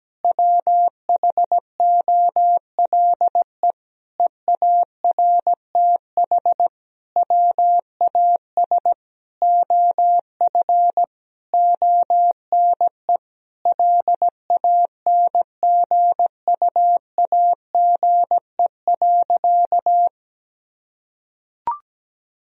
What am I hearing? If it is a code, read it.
Morse code, heard as a keyed tone, about 17 wpm, 700 Hz. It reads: WHOLE EARTH WAS OF ONE LANGUAGE.
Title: Sentences from Top 1000 Words - Review Part 3 - 17wpm